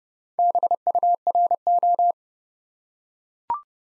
Morse code, heard as a keyed tone, about 31 wpm, 700 Hz. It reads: BURO
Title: Top 90 Abbr 30WPM